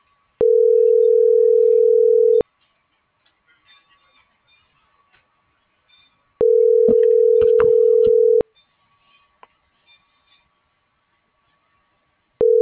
Ringback tone (North America), 440 + 480 Hz.